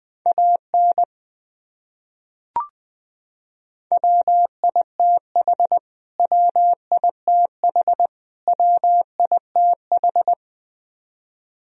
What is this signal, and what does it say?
Morse code, heard as a keyed tone, about 20 wpm, 700 Hz. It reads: AN WITH WITH WITH
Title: Top 100 Words Repeated 20wpm